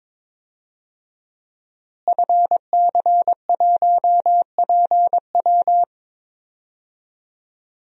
Morse code, heard as a keyed tone, about 22 wpm, 700 Hz. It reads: FC1PW
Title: DXCC-100 Call Signs - 5 Characters - Repeated 3x - 22wpm